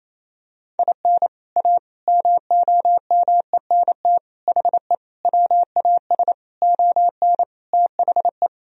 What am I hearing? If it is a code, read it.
Morse code, heard as a keyed tone, about 27 wpm, 700 Hz. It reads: IN A MOMENT HE WAS ON THE
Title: Sentences from Top 1000 Words - Review Part 2 - 28wpm